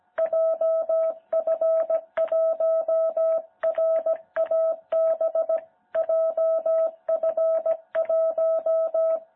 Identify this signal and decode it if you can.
Morse code, heard as a keyed tone, about 17 wpm, 660 Hz. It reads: JF1RAB JF1
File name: jh8cbh / 2021cw